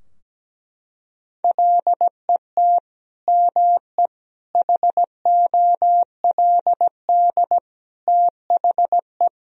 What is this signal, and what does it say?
Morse code, heard as a keyed tone, about 17 wpm, 700 Hz. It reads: LET ME HOLD THE
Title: Sentences from Top 1000 Words - Review Part 2 - 17wpm